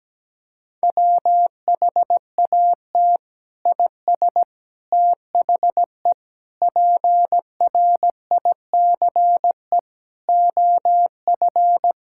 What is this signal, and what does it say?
Morse code, heard as a keyed tone, about 17 wpm, 700 Hz. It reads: WHAT IS THE PRICE OF
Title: Sentences from Top 1000 Words - Review Part 5 - 17wpm